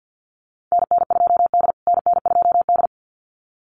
Morse code, heard as a keyed tone, about 50 wpm, 700 Hz. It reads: NN2D NN2D
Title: USA Call Signs Repeated 2x 50wpm